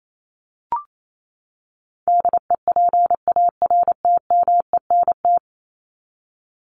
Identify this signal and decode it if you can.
Morse code, heard as a keyed tone, about 28 wpm, 700 Hz. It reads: DEPARTMENT
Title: Top 751-1000 Words 28wpm